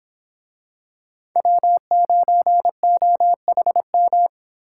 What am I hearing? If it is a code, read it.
Morse code, heard as a keyed tone, about 26 wpm, 700 Hz. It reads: W9OHM